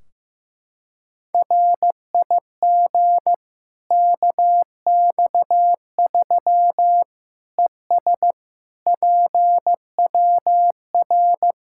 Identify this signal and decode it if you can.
Morse code, heard as a keyed tone, about 16 wpm, 700 Hz. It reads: RIG KX3 ES PWR
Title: K1USN SST QSO Practice 15wpm